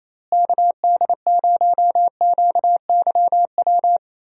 Morse code, heard as a keyed tone, about 28 wpm, 700 Hz. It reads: KD0QYW
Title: USA Call Signs - 6 Characters - Repeated 2x - 28wpm